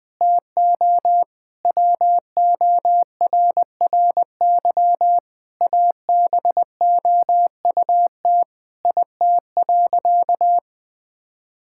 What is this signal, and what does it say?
Morse code, heard as a keyed tone, about 20 wpm, 700 Hz. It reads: TO WORRY ABOUT IT.
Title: Sentences with 5-Word Phrases - 20wpm